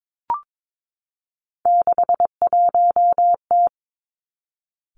Morse code, heard as a keyed tone, about 22 wpm, 700 Hz. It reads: 61T
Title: Three Challenging Letter-Number Combinations 22wpm